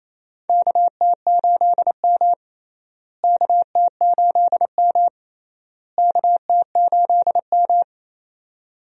Morse code, heard as a keyed tone, about 28 wpm, 700 Hz. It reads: KT8M KT8M KT8M
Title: USA Call Signs - 4 Characters - Repeated 3x - 28wpm